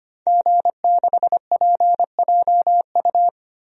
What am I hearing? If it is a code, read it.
Morse code, heard as a keyed tone, about 25 wpm, 700 Hz. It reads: G6PJU
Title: DXCC-100 Call Signs - 5 Characters - 25wpm